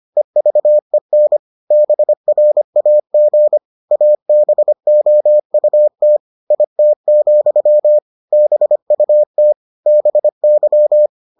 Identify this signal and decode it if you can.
Morse code, heard as a keyed tone, about 25 wpm, 600 Hz. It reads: EVEN BRAG ABOUT IT, BUT BY